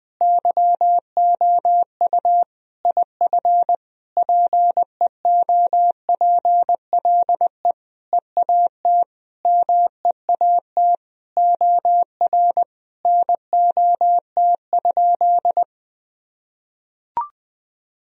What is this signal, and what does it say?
Morse code, heard as a keyed tone, about 20 wpm, 700 Hz. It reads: YOU IF PEOPLE EAT MEAT OR NOT?
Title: Sentences from Top 1000 Words - Review Part 3 - 20wpm